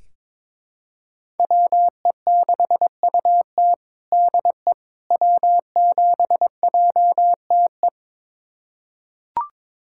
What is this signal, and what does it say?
Morse code, heard as a keyed tone, about 22 wpm, 700 Hz. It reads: WE6UT DE W7JTE
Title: USA Callsign DE Callsign - 5 Characters - 22wpm